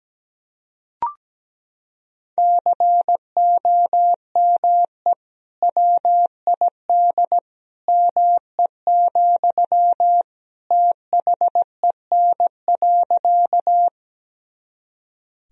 Morse code, heard as a keyed tone, about 17 wpm, 700 Hz. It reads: COME WID ME, THEN.
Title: Sentences from Top 200 Words and Abbreviations 17wpm